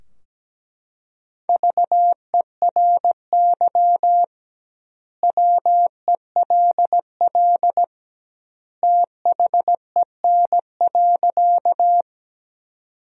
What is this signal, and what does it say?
Morse code, heard as a keyed tone, about 17 wpm, 700 Hz. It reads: VERY WELL THEN.
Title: 2x Word-Spacing - Sentences from Top 200 Words - 17wpm